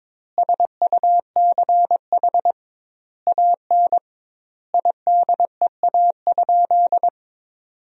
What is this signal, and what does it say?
Morse code, heard as a keyed tone, about 22 wpm, 700 Hz. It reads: SUCH AN IDEA?
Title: 2x Word-Spacing - Sentences from Top 400 Words - 22wpm